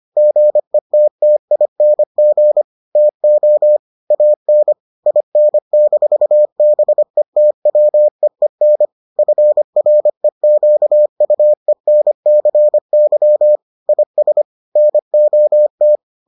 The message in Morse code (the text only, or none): GETTING TO AN IN-BETWEEN FREQUENCY IS NOT